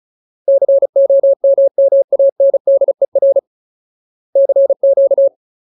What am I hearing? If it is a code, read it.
Morse code, heard as a keyed tone, about 34 wpm, 550 Hz. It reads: COMMANDER CQ
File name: Ultimate Word List / 35WPM Ultimate